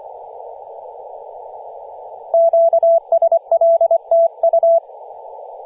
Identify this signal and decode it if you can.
Morse code, heard as a keyed tone, about 24 wpm, 670 Hz. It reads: QSL TU
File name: posts / 2016-nov-10-0528z-t30ma-21-007mhz-cw